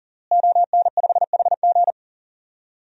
Morse code, heard as a keyed tone, about 40 wpm, 700 Hz. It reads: ON5HG
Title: DXCC-100 Call Signs - 5 Characters - Repeated 3x - 40wpm